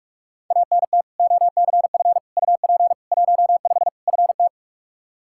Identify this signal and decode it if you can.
Morse code, heard as a keyed tone, about 46 wpm, 700 Hz. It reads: ANT OCF UP 15 FT
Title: Practice QSO - Exchange 2 - Info - 45wpm